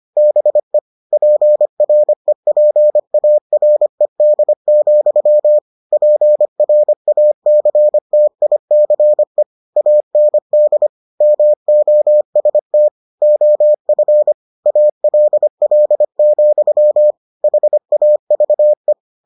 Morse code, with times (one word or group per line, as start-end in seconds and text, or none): 0.2-0.8 BE
1.1-5.6 PREPARED,
5.9-9.4 PRACTICE
9.8-10.9 AND
11.2-12.9 MOST
13.2-14.3 OF
14.7-17.1 ALL,
17.4-18.9 HAVE